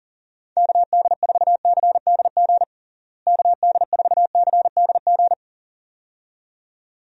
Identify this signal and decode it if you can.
Morse code, heard as a keyed tone, about 40 wpm, 700 Hz. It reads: KD4CDG KD4CDG
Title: USA Call Signs - 6 Characters - Repeated 2x - 40wpm